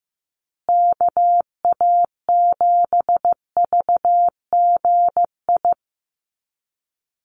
Morse code, heard as a keyed tone, about 15 wpm, 700 Hz. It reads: KA7VGI
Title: USA Call Signs Repeated 2x 15wpm